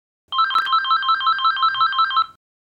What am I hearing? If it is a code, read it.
There is no code.